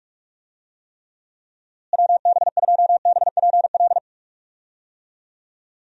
Morse code, heard as a keyed tone, about 47 wpm, 700 Hz. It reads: WB2BPL
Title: USA Call Signs - 6 Characters - Repeated 3x - 45wpm